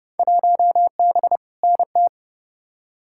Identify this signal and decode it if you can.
Morse code, heard as a keyed tone, about 30 wpm, 700 Hz. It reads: 1B NT
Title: Field Day Exchanges Encore - 30wpm